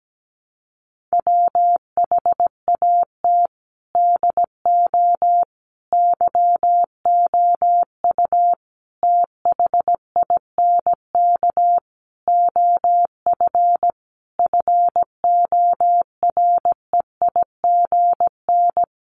Morse code, heard as a keyed tone, about 17 wpm, 700 Hz. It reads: WHAT DO YOU THINK OF FOREIGN
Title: Sentences from Top 1400 Words 17wpm